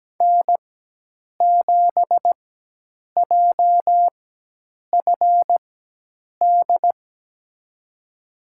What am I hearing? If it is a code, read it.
Morse code, heard as a keyed tone, about 17 wpm, spaced as for 8 wpm, 700 Hz. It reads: N7JFD
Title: USA Call Signs - 5 Characters - 4x Character Spacing - 17wpm